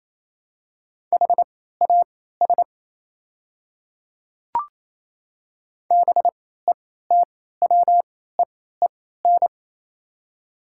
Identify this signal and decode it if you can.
Morse code, heard as a keyed tone, about 27 wpm, spaced as for 16 wpm, 700 Hz. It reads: HAS BETWEEN
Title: 3x Character Spacing - Top 300 Words - 28wpm